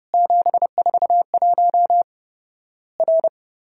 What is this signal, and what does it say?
Morse code, heard as a keyed tone, about 31 wpm, 700 Hz. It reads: 741 R
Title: Worked All Europe DX Contest - QTC Exchange - 30wpm